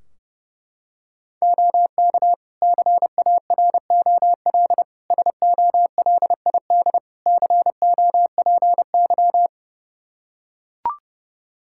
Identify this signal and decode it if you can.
Morse code, heard as a keyed tone, about 30 wpm, 700 Hz. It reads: OK CAROL SOLID COPY
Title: K1USN SST QSO Practice 30wpm